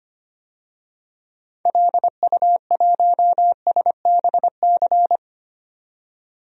Morse code, heard as a keyed tone, about 25 wpm, 700 Hz. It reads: LU1SBC
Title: DXCC-100 Call Signs - 6 Characters - 25wpm